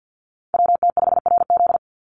Morse code, heard as a keyed tone, about 49 wpm, 700 Hz. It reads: WT5RZ
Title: Call Signs Repeated 3x - 50wpm